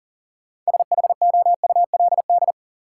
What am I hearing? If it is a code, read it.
Morse code, heard as a keyed tone, about 40 wpm, 700 Hz. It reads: SHOULD
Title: Sets of 2 Words - New Word Drill - 40wpm